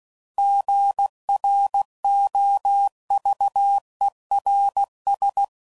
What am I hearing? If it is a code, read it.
Morse code, heard as a keyed tone, about 16 wpm, 800 Hz. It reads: GROVERS
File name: CW Code Groups @15WPM / fromSM's / Cities